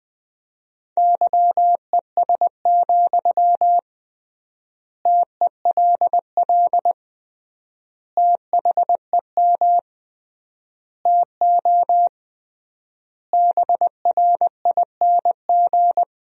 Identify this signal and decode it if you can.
Morse code, heard as a keyed tone, about 20 wpm, 700 Hz. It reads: YES, TELL THEM TO BRING
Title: 3x Word-Spacing - Sentences from Top 500 Words - 20wpm